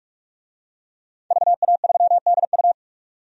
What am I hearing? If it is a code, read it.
Morse code, heard as a keyed tone, about 46 wpm, 700 Hz. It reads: VA3DU
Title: DXCC-100 Call Signs - 5 Characters - 45wpm